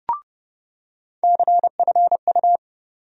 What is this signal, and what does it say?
Morse code, heard as a keyed tone, about 31 wpm, 700 Hz. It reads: CFU
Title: USA Call Sign Postfixes - 3 Letters 30wpm